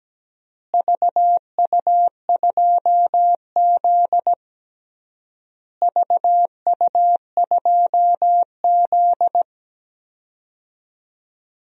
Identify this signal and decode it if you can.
Morse code, heard as a keyed tone, about 17 wpm, 700 Hz. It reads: VU2Z VU2Z
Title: DXCC-100 Call Signs - 4 Characters - Repeated 3x - 17wpm